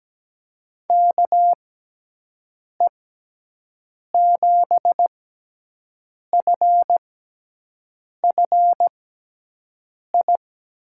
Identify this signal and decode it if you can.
Morse code, heard as a keyed tone, about 17 wpm, spaced as for 6 wpm, 700 Hz. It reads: KE7FFI
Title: USA Call Signs - 6 Characters - 6x Character Spacing - 17wpm